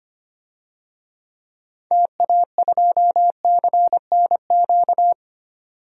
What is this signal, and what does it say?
Morse code, heard as a keyed tone, about 25 wpm, 700 Hz. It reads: TA2CNQ
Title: DXCC-100 Call Signs - 6 Characters - 25wpm